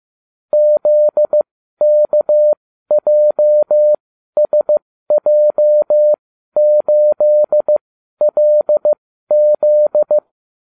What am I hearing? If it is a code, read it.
Morse code, heard as a keyed tone, about 15 wpm, spaced as for 12 wpm, 600 Hz. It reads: ZKJSJ8LZ